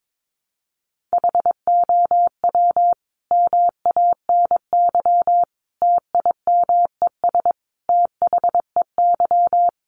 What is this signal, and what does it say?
Morse code, heard as a keyed tone, about 22 wpm, 700 Hz. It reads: HOW MANY TIMES THEY
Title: A Sound Of Thunder 22wpm